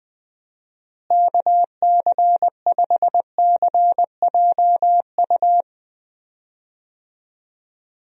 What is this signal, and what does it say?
Morse code, heard as a keyed tone, about 20 wpm, 700 Hz. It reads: KC5CJU
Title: USA Call Signs - 6 Characters - Repeated 3x - 20wpm